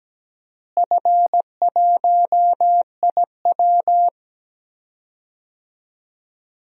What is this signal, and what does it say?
Morse code, heard as a keyed tone, about 17 wpm, 700 Hz. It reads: F1IW
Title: DXCC-100 Call Signs - 4 Characters - Repeated 3x - 17wpm